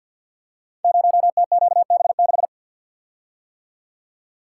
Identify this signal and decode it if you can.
Morse code, heard as a keyed tone, about 48 wpm, 700 Hz. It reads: 0TQB6
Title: Code Groups - 5 Letter:Numbers - 50wpm